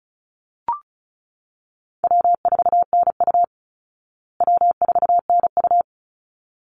Morse code, heard as a keyed tone, about 34 wpm, 700 Hz. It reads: W4NU W4NU
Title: Easy Call Signs Repeated 2x - 35wpm